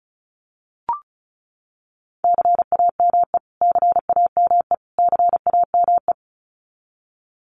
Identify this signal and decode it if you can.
Morse code, heard as a keyed tone, about 34 wpm, 700 Hz. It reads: CAME CAME CAME
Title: Top 500 Words Repeated 35wpm